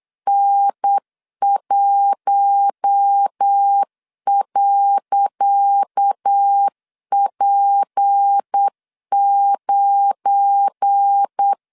Morse code, heard as a keyed tone, about 8 wpm, 800 Hz. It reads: N1.P9
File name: LEZIONI / 10 SIGNS / 10C DOT 8wpm 3649